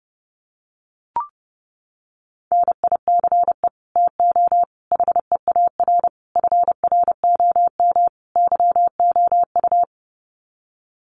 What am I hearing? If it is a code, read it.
Morse code, heard as a keyed tone, about 30 wpm, 700 Hz. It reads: NICE TO HEAR FROM YOU
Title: Common QSO Phrases 30wpm